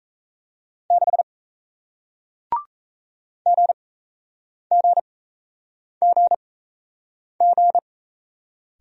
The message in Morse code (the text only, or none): 6 GGGG